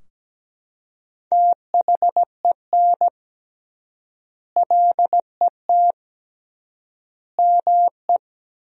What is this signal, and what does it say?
Morse code, heard as a keyed tone, about 17 wpm, 700 Hz. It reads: THEN LET ME
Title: 3x Word-Spacing - Sentences from Top 300 Words - 17wpm